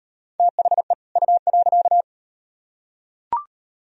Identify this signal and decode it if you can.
Morse code, heard as a keyed tone, about 34 wpm, 700 Hz. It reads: THE U.